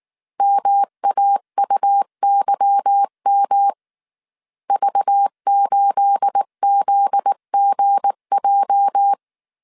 Morse code, heard as a keyed tone, about 19 wpm, 800 Hz. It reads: MAUYM V8ZGJ
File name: CW / DET / 1 9 1500C S B 18wpm 2551